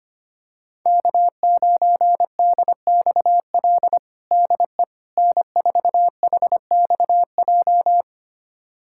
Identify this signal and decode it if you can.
Morse code, heard as a keyed tone, about 25 wpm, 700 Hz. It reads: K9DXL DE N4HXJ
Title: USA Callsign DE Callsign - 5 Characters - 25wpm